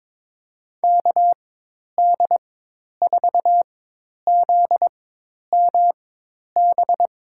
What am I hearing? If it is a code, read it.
Morse code, heard as a keyed tone, about 21 wpm, spaced as for 10 wpm, 700 Hz. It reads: KD4ZMB
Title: USA Call Signs - 4x Character Spacing - 22wpm